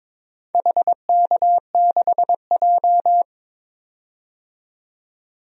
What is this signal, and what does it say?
Morse code, heard as a keyed tone, about 22 wpm, 700 Hz. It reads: HK6J
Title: DXCC-100 Call Signs - 4 Characters - 22wpm